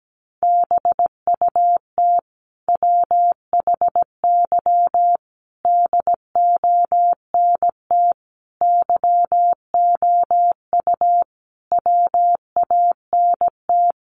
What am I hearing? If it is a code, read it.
Morse code, heard as a keyed tone, about 17 wpm, 700 Hz. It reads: BUT WHY DONT YOU WANT
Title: Sentences from Top 300 Words 17wpm